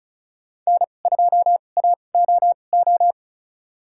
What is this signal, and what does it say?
Morse code, heard as a keyed tone, about 36 wpm, spaced as for 25 wpm, 700 Hz. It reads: N2AOO
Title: USA Call Signs - 5 Characters - 2x Character Spacing - 35wpm